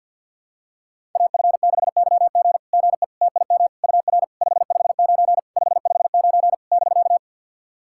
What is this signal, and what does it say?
Morse code, heard as a keyed tone, about 50 wpm, 700 Hz. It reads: AF6YG GE TIM UR 559 559 <BK>
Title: SOTA QSO Practice 50wpm